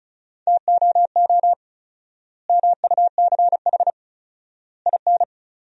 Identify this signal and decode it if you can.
Morse code, heard as a keyed tone, about 36 wpm, 700 Hz. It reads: TOO MUCH IN
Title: 4x Word-Spacing - Sentences from Top 200 Words - 35wpm